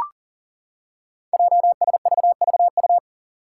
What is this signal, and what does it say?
Morse code, heard as a keyed tone, about 40 wpm, 700 Hz. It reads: JSVVU
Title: Five Challenging Letter-Number Combinations 40wpm